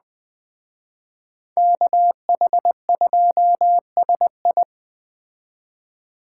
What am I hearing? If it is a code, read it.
Morse code, heard as a keyed tone, about 20 wpm, 700 Hz. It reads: KH2SI
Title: Speed Racing - USA 2x2 Call Signs 17wpm